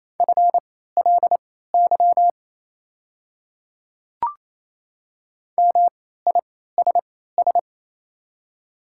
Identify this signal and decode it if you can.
Morse code, heard as a keyed tone, about 28 wpm, spaced as for 16 wpm, 690 Hz. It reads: FLY MISS
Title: 3x Character Spacing - Top 500 Words - 28wpm